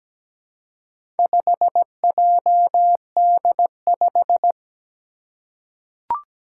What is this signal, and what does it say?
Morse code, heard as a keyed tone, about 17 wpm, 700 Hz. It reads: 5JD5
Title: Four Challenging Letter-Number Combinations 17wpm